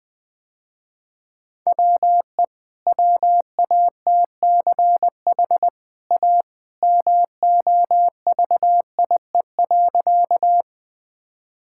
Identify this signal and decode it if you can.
Morse code, heard as a keyed tone, about 20 wpm, 700 Hz. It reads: WE WATCH A MOVIE.
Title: Easy Sentences with Hints - 20wpm